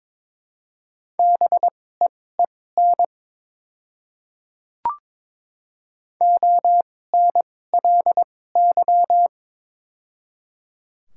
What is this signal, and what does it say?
Morse code, heard as a keyed tone, about 22 wpm, spaced as for 16 wpm, 700 Hz. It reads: BEEN ONLY
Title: 2x Character Spacing - Top 200 Words - 22wpm